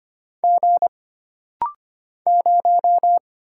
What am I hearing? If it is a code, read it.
Morse code, heard as a keyed tone, about 25 wpm, 700 Hz. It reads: G 0
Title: Single Letter-Number Alt - Rapid-Fire 25wpm